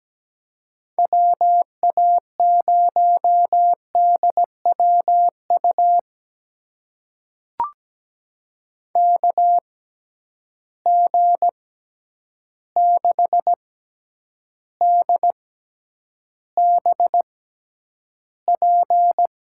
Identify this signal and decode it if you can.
Morse code, heard as a keyed tone, about 17 wpm, spaced as for 9 wpm, 700 Hz. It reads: WA0DWU KG6DBP
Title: USA Call Signs - 6 Characters - 6x Character Spacing - 17wpm